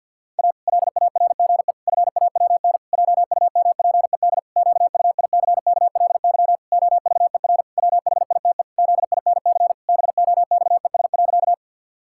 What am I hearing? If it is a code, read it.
Morse code, heard as a keyed tone, about 50 wpm, 700 Hz. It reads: A LARGE FAWN JUMPED QUICKLY OVER WHITE ZINC BOXES.
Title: Pangram Challenge v2 - 50wpm